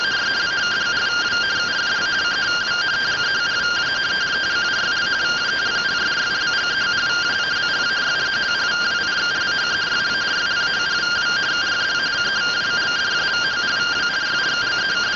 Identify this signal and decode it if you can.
Two-tone teleprinter signal, about 170 Hz shift.